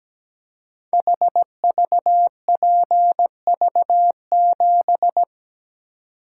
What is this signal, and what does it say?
Morse code, heard as a keyed tone, about 17 wpm, 700 Hz. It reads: HVPV7
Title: Code Groups - 5 Letter:Numbers - 17wpm